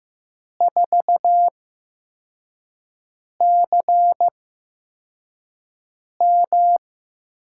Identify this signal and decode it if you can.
Morse code, heard as a keyed tone, about 15 wpm, spaced as for 4 wpm, 700 Hz. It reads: 4CM
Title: USA Call Signs - 5 Characters - 8x Character Spacing - 15wpm